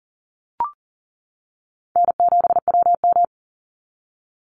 Morse code, heard as a keyed tone, about 40 wpm, 700 Hz. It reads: N7WM
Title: NA SOTA Chaser Callsigns 40wpm